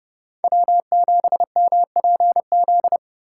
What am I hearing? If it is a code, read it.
Morse code, heard as a keyed tone, about 31 wpm, 700 Hz. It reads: W7MPZ
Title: USA Call Signs - 5 Characters - 3x Character Spacing - 30wpm